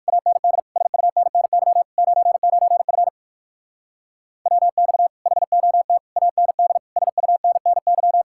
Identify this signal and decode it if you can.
Morse code, heard as a keyed tone, about 59 wpm, 700 Hz. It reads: AND SUNNY 99F WX HOT AND SUNNY